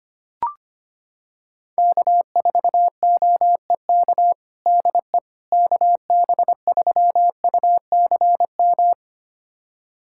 Morse code, heard as a keyed tone, about 25 wpm, 700 Hz. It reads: K4OEK DE KB3UCM
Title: USA Callsign DE Callsign 25wpm